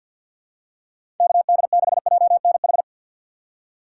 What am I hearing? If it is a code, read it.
Morse code, heard as a keyed tone, about 52 wpm, 700 Hz. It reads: KD6JNH
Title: USA Call Signs - 3x Character Spacing - 50wpm